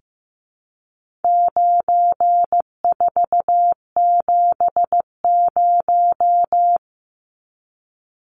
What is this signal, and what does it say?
Morse code, heard as a keyed tone, about 15 wpm, 700 Hz. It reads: 9470
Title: Numbers - 4-Digit 15wpm